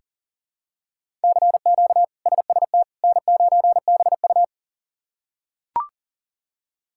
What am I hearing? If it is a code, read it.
Morse code, heard as a keyed tone, about 40 wpm, 700 Hz. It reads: CQ SST N9BU
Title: K1USN SST QSO Practice - 40wpm